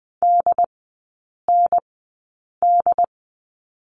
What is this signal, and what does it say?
Morse code, heard as a keyed tone, about 20 wpm, spaced as for 8 wpm, 700 Hz. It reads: DND
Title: 107 Lesson 18e - Letter m - Three Characters 20wpm